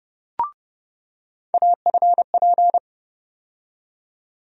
Morse code, heard as a keyed tone, about 30 wpm, 700 Hz. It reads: AFP
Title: USA Call Sign Postfixes - 3 Letters 30wpm